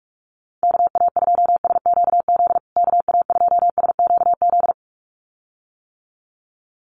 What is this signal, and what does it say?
Morse code, heard as a keyed tone, about 44 wpm, 700 Hz. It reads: KA2SQZ KA2SQZ
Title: USA Call Signs Repeated 2x 45wpm